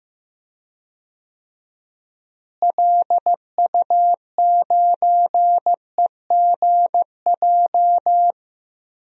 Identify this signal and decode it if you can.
Morse code, heard as a keyed tone, about 15 wpm, 700 Hz. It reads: LU9EGJ
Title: DXCC-100 Call Signs - 6 Characters - Repeated 2x - 15wpm